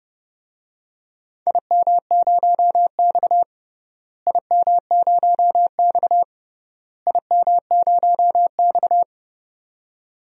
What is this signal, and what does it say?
Morse code, heard as a keyed tone, about 30 wpm, 700 Hz. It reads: IM0X IM0X IM0X
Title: DXCC-100 Call Signs - 4 Characters - Repeated 3x - 30wpm